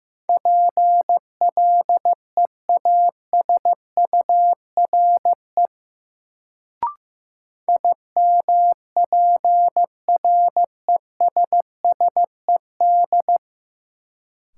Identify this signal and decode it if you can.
Morse code, heard as a keyed tone, about 15 wpm, 700 Hz. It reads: PLEASURE IMPRESSED
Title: Sentences from Top 1600 Words - New Word Drill Alt - 15wpm